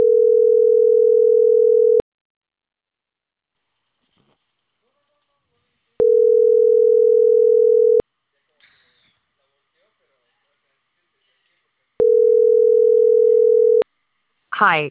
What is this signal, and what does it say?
Ringback tone (North America), 440 + 480 Hz, with speech.
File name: CENTRAL CAEA / out agcaea03